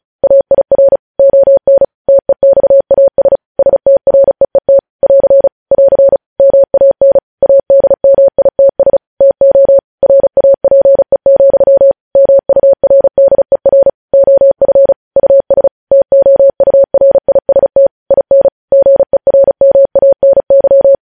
Morse code, heard as a keyed tone, about 36 wpm, 550 Hz. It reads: AIR ON TEXAS STREET <AR> <AR> MAN ADMITS TO RAPE, MURDER OF US TOURIST IN GERMANY
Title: News Headlines at 35